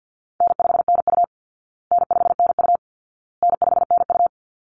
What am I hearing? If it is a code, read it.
Morse code, heard as a keyed tone, about 50 wpm, 700 Hz. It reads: N5NU N5NU N5NU
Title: Easy Call Signs Repeated 3x - 50wpm